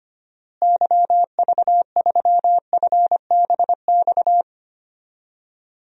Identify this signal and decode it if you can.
Morse code, heard as a keyed tone, about 25 wpm, 700 Hz. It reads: YV3FBX
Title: DXCC-100 Call Signs - 6 Characters - Repeated 3x - 25wpm